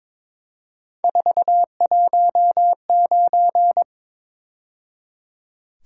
Morse code, heard as a keyed tone, about 26 wpm, 700 Hz. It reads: 419
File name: Speed-Racing - Three Letter-Number Combinations - 22wpm